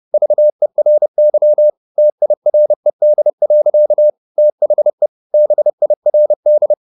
Morse code, heard as a keyed tone, about 30 wpm, 600 Hz. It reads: VERY TIRED. THE BIRD
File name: the old man and the sea / the old man and the sea 1 30wpm